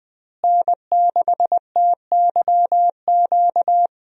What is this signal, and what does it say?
Morse code, heard as a keyed tone, about 20 wpm, 700 Hz. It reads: N6TYQ
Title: USA Call Signs - 5 Characters - Repeated 2x - 20wpm